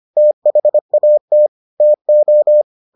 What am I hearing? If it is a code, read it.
Morse code, heard as a keyed tone, about 25 wpm, 600 Hz. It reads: THAT TO